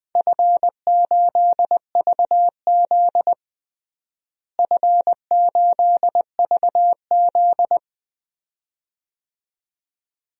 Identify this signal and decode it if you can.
Morse code, heard as a keyed tone, about 20 wpm, 700 Hz. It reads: F8VZ F8VZ
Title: DXCC-100 Call Signs - 4 Characters - Repeated 2x - 20wpm